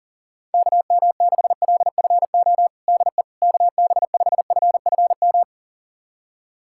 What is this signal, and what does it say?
Morse code, heard as a keyed tone, about 40 wpm, 700 Hz. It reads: KM6LFO DE KB5FFM
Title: USA Callsign DE Callsign - 6 Characters - 40wpm